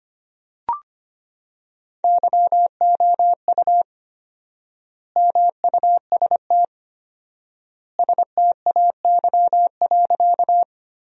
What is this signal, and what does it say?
Morse code, heard as a keyed tone, about 25 wpm, 700 Hz. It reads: YOU MUST STAY.
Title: 4x Word-Spacing - Sentences from Top 500 Words - 25wpm